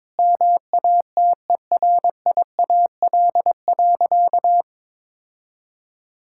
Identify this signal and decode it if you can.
Morse code, heard as a keyed tone, about 22 wpm, 700 Hz. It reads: MATERIAL.
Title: Sentences from Top 1000 Words - Review Part 5 - 22wpm